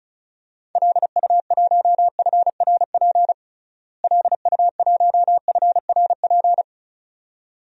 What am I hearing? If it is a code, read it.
Morse code, heard as a keyed tone, about 36 wpm, 700 Hz. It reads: LU1FRP LU1FRP
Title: DXCC-100 Call Signs - 6 Characters - Repeated 2x - 35wpm